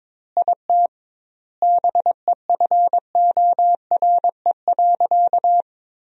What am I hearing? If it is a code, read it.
Morse code, heard as a keyed tone, about 22 wpm, 700 Hz. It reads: IT BEFORE.
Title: 2x Word-Spacing - Sentences from Top 400 Words - 22wpm